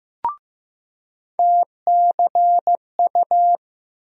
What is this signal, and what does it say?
Morse code, heard as a keyed tone, about 15 wpm, 700 Hz. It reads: TCU
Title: USA Call Sign Postfixes - 3 Letters 15wpm